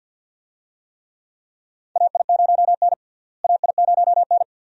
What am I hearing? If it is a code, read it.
Morse code, heard as a keyed tone, about 52 wpm, 700 Hz. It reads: AI0N AI0N
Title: USA Call Signs - 4 Characters - Repeated 2x - 50wpm